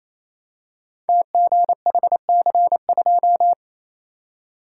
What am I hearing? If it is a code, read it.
Morse code, heard as a keyed tone, about 28 wpm, 700 Hz. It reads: TGHC2
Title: Code Groups - 5 Letter:Numbers - 28wpm